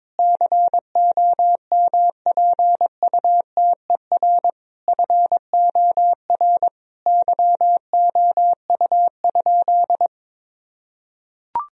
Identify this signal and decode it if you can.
Morse code, heard as a keyed tone, about 22 wpm, 700 Hz. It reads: COMPUTER FOR YOU?
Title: Sentences from Top 1000 Words - Review Part 3 - 22wpm